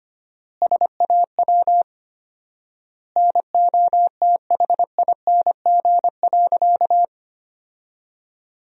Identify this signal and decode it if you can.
Morse code, heard as a keyed tone, about 25 wpm, 700 Hz. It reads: SAW NOTHING.
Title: 4x Word-Spacing - Sentences from Top 500 Words - 25wpm